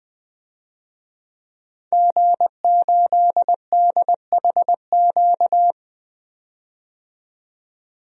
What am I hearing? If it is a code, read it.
Morse code, heard as a keyed tone, about 20 wpm, 700 Hz. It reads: G8DHQ
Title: DXCC-100 Call Signs - 5 Characters - 20wpm